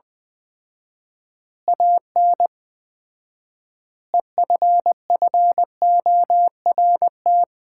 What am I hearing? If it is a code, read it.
Morse code, heard as a keyed tone, about 20 wpm, 700 Hz. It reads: AN EFFORT
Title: 4x Word-Spacing - Sets of 4 Words - 20wpm